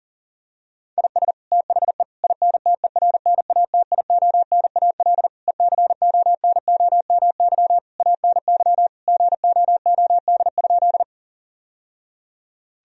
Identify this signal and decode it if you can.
Morse code, heard as a keyed tone, about 37 wpm, 700 Hz. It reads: IS THE INTERNATIONAL ECONOMY ANY GOOD?
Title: Sentences from Top 1000 Words - Review Part 4 - 40wpm